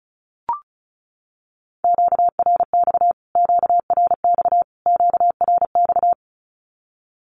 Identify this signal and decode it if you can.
Morse code, heard as a keyed tone, about 34 wpm, 700 Hz. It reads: QRX QRX QRX
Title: Q-Codes Repeated 3x 35wpm